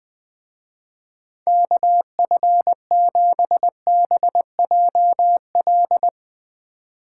Morse code, heard as a keyed tone, about 21 wpm, 700 Hz. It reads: KF7BJL
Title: Speed Racing - USA 2x3 Call Signs 20wpm